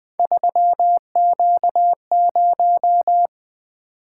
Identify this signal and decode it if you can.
Morse code, heard as a keyed tone, about 20 wpm, 700 Hz. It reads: 3Q0
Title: Three Letter-Number Combinations - 20wpm